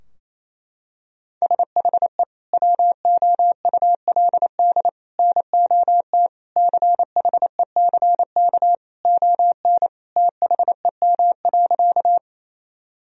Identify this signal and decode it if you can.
Morse code, heard as a keyed tone, about 28 wpm, 700 Hz. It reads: SHE WOULD NOT CHECK ON THEM.
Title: Sentences from Top 1000 Words - Review Part 3 - 28wpm